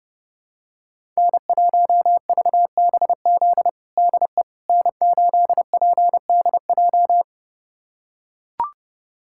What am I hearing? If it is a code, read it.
Morse code, heard as a keyed tone, about 31 wpm, 700 Hz. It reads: N1VBZ DE N8PDJ
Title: USA Callsign DE Callsign - 5 Characters - 30wpm